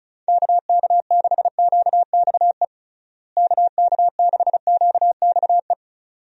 Morse code, heard as a keyed tone, about 36 wpm, 700 Hz. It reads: KK6QXE KK6QXE
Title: USA Call Signs - 6 Characters - Repeated 2x - 35wpm